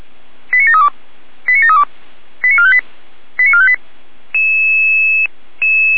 Five-tone selective call (ZVEI), heard as quick runs of five tones, tones of about 70 ms. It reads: 89732 89732 89458 89458